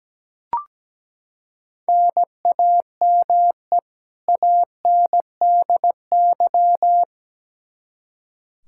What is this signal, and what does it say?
Morse code, heard as a keyed tone, about 17 wpm, 700 Hz. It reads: NAME ANDY
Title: Practice QSO - Exchange 1 - Beginning - 17wpm